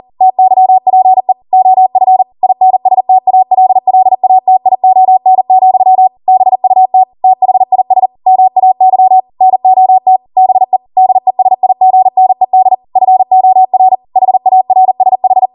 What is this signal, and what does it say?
Morse code, heard as a keyed tone, about 40 wpm, 750 Hz. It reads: TYPE OF INSTALLATION, BUT THIS MAY NOT BE DESIGNED FOR HARSH
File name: Morse / 40WPM